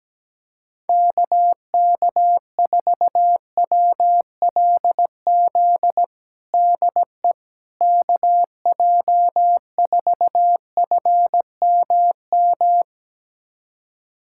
Morse code, heard as a keyed tone, about 17 wpm, 700 Hz. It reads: KK4WLZ DE KJ4FMM
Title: USA Callsign DE Callsign - 6 Characters - 17wpm